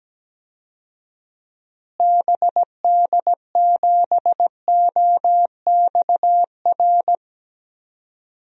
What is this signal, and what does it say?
Morse code, heard as a keyed tone, about 17 wpm, 700 Hz. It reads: BD7OXR
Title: DXCC-100 Call Signs - 6 Characters - Repeated 2x - 17wpm